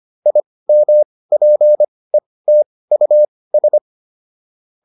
Morse code, heard as a keyed tone, about 25 wpm, spaced as for 18 wpm, 600 Hz. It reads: IMPETUS